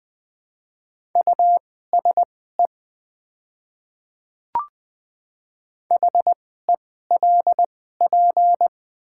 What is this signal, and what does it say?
Morse code, heard as a keyed tone, about 20 wpm, spaced as for 14 wpm, 700 Hz. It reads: USE HELP
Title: 2x Character Spacing - Top 200 Words - 20wpm